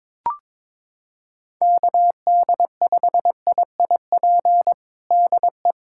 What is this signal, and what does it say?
Morse code, heard as a keyed tone, about 22 wpm, 700 Hz. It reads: KD5IIP DE
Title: USA Callsign DE Callsign - 6 Characters - 22wpm